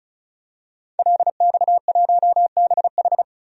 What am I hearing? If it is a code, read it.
Morse code, heard as a keyed tone, about 34 wpm, 700 Hz. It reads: LX1BH
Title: DXCC-100 Call Signs - 5 Characters - 35wpm